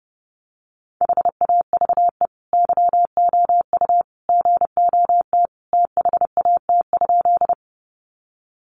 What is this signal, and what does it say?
Morse code, heard as a keyed tone, about 28 wpm, 700 Hz. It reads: HAVE YOU GOT THAT?
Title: Sentences from Top 300 Words 30wpm